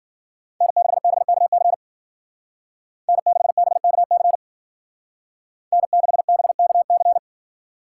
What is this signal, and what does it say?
Morse code, heard as a keyed tone, about 48 wpm, 700 Hz. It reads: N6BKC N6BKC N6BKC
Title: Speed Racing - USA Call Signs 40wpm